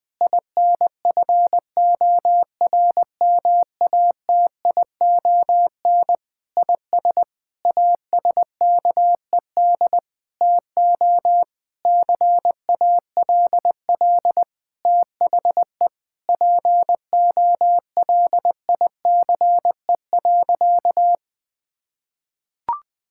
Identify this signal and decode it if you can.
Morse code, heard as a keyed tone, about 20 wpm, 700 Hz. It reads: INFORMATION IS ASKED TO CALL THE POLICE.
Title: Sentences with 7-Word Phrases - 20wpm